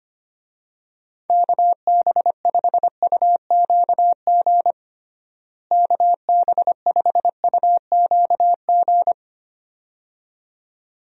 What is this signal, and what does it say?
Morse code, heard as a keyed tone, about 25 wpm, 700 Hz. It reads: KB5UQG KB5UQG
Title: USA Call Signs - 6 Characters - Repeated 2x - 25wpm